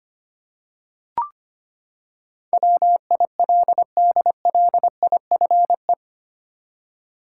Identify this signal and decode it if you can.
Morse code, heard as a keyed tone, about 25 wpm, 700 Hz. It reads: WILDLIFE
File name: Eight Letter Words - 25wpm